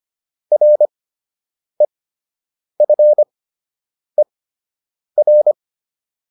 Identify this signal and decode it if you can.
Morse code, heard as a keyed tone, about 25 wpm, spaced as for 8 wpm, 600 Hz. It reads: REFER